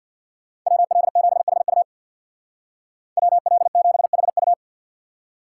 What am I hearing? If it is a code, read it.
Morse code, heard as a keyed tone, about 60 wpm, 700 Hz. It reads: WL7HU WL7HU
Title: Speed Racing - USA Call Signs 50wpm